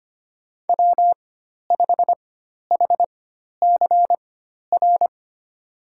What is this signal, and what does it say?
Morse code, heard as a keyed tone, about 25 wpm, spaced as for 12 wpm, 700 Hz. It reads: W5HCR
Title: USA Call Signs - 5 Characters - 4x Character Spacing - 25wpm